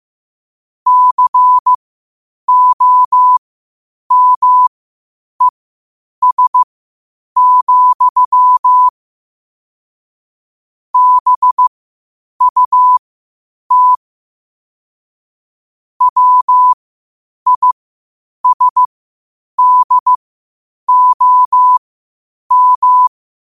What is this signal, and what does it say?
Morse code, heard as a keyed tone, about 15 wpm, spaced as for 8 wpm, 1000 Hz. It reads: COMES, BUT WISDOM
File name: qotdfiles / 8wpm